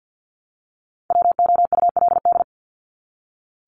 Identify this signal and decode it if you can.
Morse code, heard as a keyed tone, about 48 wpm, 700 Hz. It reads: WOULD
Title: Top 200 Words 50wpm